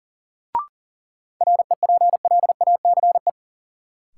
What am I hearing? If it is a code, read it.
Morse code, heard as a keyed tone, about 40 wpm, 700 Hz. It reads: REPLACE
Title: Sentences from Top 1400 Words - New Word Drill Alt - 40wpm